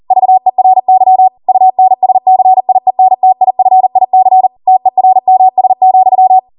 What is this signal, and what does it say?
Morse code, heard as a keyed tone, about 40 wpm, 750 Hz. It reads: VERY UNSCIENTIFIC TERMS,